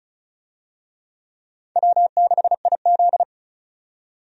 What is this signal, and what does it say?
Morse code, heard as a keyed tone, about 34 wpm, 700 Hz. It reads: W6IZ
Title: USA Call Signs - 4 Characters - 35wpm